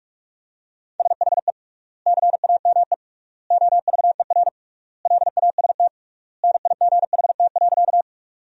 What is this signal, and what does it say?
Morse code, heard as a keyed tone, about 42 wpm, 700 Hz. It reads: SHE CAME OVER LAST NIGHT.
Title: 3x Word-Spacing - Sentences from Top 300 Words - 45wpm